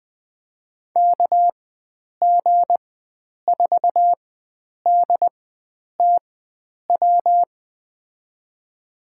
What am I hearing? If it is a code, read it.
Morse code, heard as a keyed tone, about 20 wpm, spaced as for 9 wpm, 700 Hz. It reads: KG4DTW
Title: USA Call Signs - 6 Characters - 4x Character Spacing - 20wpm